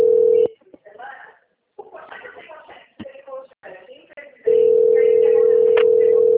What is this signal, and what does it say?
Ringback tone (North America), 440 + 480 Hz, with speech.
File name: out- - - - - .13466